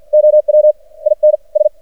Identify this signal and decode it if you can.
Morse code, heard as a keyed tone, about 53 wpm, spaced as for 46 wpm, 610 Hz. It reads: OW IN S
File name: archived / CRSnet